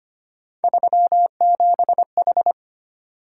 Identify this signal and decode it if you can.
Morse code, heard as a keyed tone, about 25 wpm, 700 Hz. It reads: 37H
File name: Three Letter-Number Combinations - 25wpm